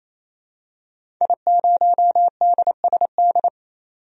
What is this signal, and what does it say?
Morse code, heard as a keyed tone, about 28 wpm, 700 Hz. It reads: I0DSD
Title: DXCC-100 Call Signs - 5 Characters - 28wpm